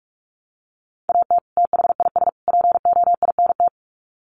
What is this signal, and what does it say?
Morse code, heard as a keyed tone, about 44 wpm, 700 Hz. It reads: AT THIS POINT
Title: Sets of 3 Words 45wpm